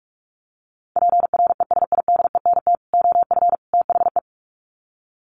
Morse code, heard as a keyed tone, about 44 wpm, 700 Hz. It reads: PRESIDENT OF THE
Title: Sets of 3 Words 45wpm